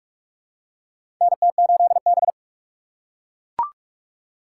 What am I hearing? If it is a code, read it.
Morse code, heard as a keyed tone, about 44 wpm, 700 Hz. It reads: NT8B
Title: USA Call Signs - 4 Characters - 6x Character Spacing - 45wpm